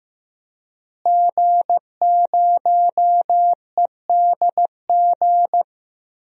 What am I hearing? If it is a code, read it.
Morse code, heard as a keyed tone, about 16 wpm, 700 Hz. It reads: G0EDG
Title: DXCC-100 Call Signs - 5 Characters - Repeated 2x - 15wpm